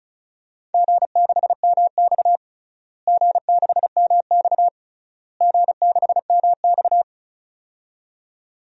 Morse code, heard as a keyed tone, about 36 wpm, 700 Hz. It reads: G6MX G6MX G6MX
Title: DXCC-100 Call Signs - 4 Characters - Repeated 3x - 35wpm